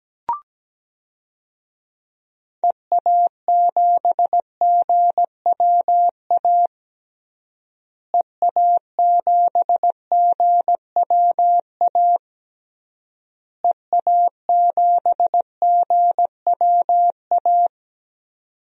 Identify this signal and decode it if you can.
Morse code, heard as a keyed tone, about 17 wpm, 700 Hz. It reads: EA7GWA EA7GWA EA7GWA
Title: DXCC-100 Call Signs - 6 Characters - Repeated 3x - 17wpm